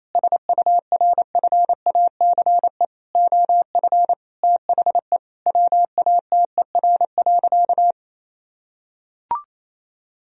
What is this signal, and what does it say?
Morse code, heard as a keyed tone, about 28 wpm, 700 Hz. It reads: SURFACE OF THE WATER.
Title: Sentences from Top 1000 Words - Review Part 3 - 28wpm